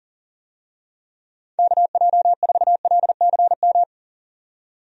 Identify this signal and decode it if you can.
Morse code, heard as a keyed tone, about 39 wpm, 700 Hz. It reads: KJ4LCM
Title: USA Call Signs - 6 Characters - 40wpm